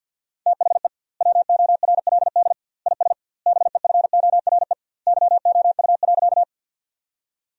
Morse code, heard as a keyed tone, about 52 wpm, 700 Hz. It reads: THE WORLD IS BEFORE YOU.
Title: 2x Word-Spacing - Sentences from Top 200 Words - 50wpm